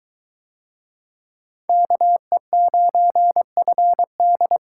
Morse code, heard as a keyed tone, about 21 wpm, 700 Hz. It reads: KE9FD